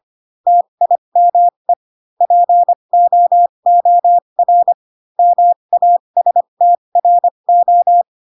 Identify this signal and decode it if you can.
Morse code, heard as a keyed tone, about 25 wpm, spaced as for 22 wpm, 700 Hz. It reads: TIME POOR MASTRO